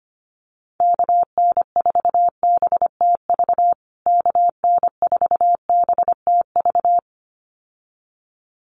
Morse code, heard as a keyed tone, about 25 wpm, 700 Hz. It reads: KN4BTV KN4BTV
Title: USA Call Signs Repeated 2x 25wpm